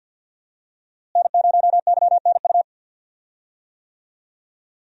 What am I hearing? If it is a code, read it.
Morse code, heard as a keyed tone, about 52 wpm, 700 Hz. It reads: N0YNU
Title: USA Call Signs - 5 Characters - Repeated 2x - 50wpm